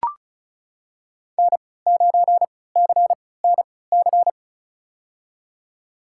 Morse code, heard as a keyed tone, about 34 wpm, spaced as for 20 wpm, 700 Hz. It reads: N9CNC
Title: USA Call Signs - 5 Characters - 3x Character Spacing - 35wpm